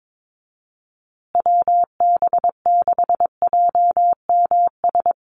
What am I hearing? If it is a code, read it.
Morse code, heard as a keyed tone, about 22 wpm, 700 Hz. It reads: WB6JMS
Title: USA Call Signs Repeated 2x 22wpm